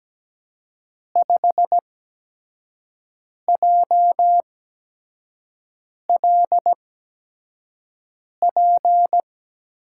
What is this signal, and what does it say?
Morse code, heard as a keyed tone, about 17 wpm, spaced as for 5 wpm, 700 Hz. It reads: 5JLP
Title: USA Call Signs - 6 Characters - 8x Character Spacing - 17wpm